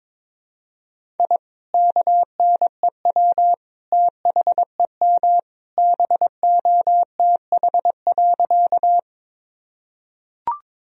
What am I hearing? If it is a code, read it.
Morse code, heard as a keyed tone, about 22 wpm, 700 Hz. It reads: I KNEW THEM BOTH.
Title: Sentences from Top 1000 Words - Review Part 1 - 22wpm